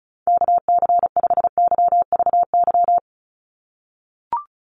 Morse code, heard as a keyed tone, about 34 wpm, 700 Hz. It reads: KC5YVY
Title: USA Call Signs 35wpm